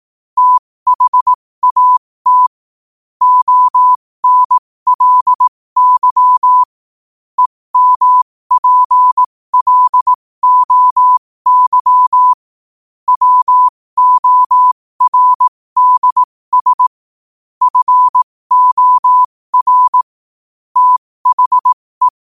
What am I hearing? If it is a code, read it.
Morse code, heard as a keyed tone, about 18 wpm, spaced as for 15 wpm, 1000 Hz. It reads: THAT ONLY EMPLOY WORDS FOR THE
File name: qotdfiles / 15wpm